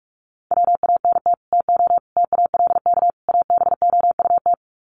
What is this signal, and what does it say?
Morse code, heard as a keyed tone, about 44 wpm, 700 Hz. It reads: WANT TO TALK ABOUT
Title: Sets of 4 Words 45wpm